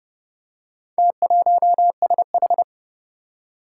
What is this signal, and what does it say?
Morse code, heard as a keyed tone, about 28 wpm, 700 Hz. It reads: T1SH
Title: Four Challenging Letter-Number Combinations 30wpm